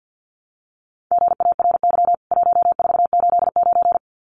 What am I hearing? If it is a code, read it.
Morse code, heard as a keyed tone, about 50 wpm, 700 Hz. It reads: GARY 1489
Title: CWT Exchange 50wpm